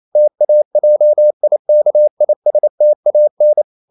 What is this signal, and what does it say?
Morse code, heard as a keyed tone, about 28 wpm, 600 Hz. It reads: TAJIKISTAN
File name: CW / Practice / NationalCapitals28